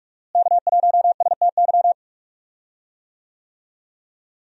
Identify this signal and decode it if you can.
Morse code, heard as a keyed tone, about 43 wpm, 700 Hz. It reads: K1STY
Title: USA Call Signs - 5 Characters - Repeated 3x - 45wpm